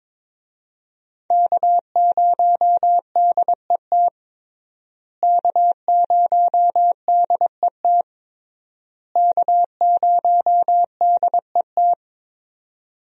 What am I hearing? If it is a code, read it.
Morse code, heard as a keyed tone, about 22 wpm, 700 Hz. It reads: K0DET K0DET K0DET
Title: USA Call Signs - 5 Characters - Repeated 3x - 22wpm